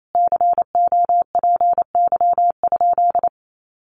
Morse code, heard as a keyed tone, about 28 wpm, 700 Hz. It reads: COPY?
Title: Sentences from Top 1400 Words 28wpm